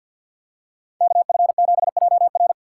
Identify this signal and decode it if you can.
Morse code, heard as a keyed tone, about 51 wpm, 700 Hz. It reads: KF7JR
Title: USA Call Signs - 5 Characters - 3x Character Spacing - 50wpm